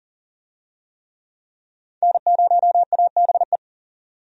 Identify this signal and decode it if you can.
Morse code, heard as a keyed tone, about 39 wpm, 700 Hz. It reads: N0ABE
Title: USA Call Signs - 5 Characters - 40wpm